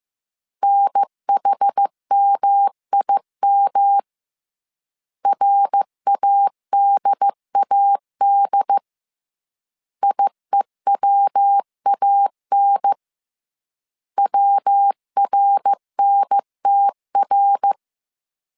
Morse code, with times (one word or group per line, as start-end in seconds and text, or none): 0.6-4.0 NHMIM
5.3-8.8 RADAD
10.0-12.9 IEWAN
14.2-17.7 WRNTR